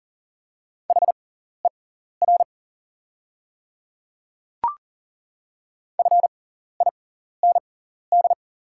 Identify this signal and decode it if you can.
Morse code, heard as a keyed tone, about 40 wpm, spaced as for 14 wpm, 690 Hz. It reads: HER FIND
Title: 6x Character Spacing - Top 100 Words - 40wpm